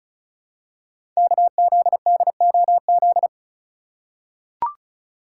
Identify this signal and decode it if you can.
Morse code, heard as a keyed tone, about 36 wpm, 700 Hz. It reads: KZDOZ
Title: Code Groups - 5 Letter:Numbers - 35wpm